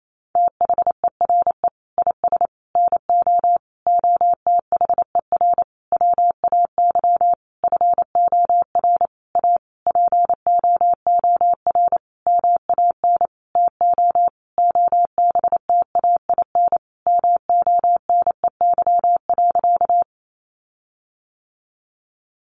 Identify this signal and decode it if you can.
Morse code, heard as a keyed tone, about 28 wpm, 700 Hz. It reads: THERE IS NO OTHER WAY FOR A POOR MAN TO OBTAIN MONEY.
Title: Sentences from Top 1300 Words 28wpm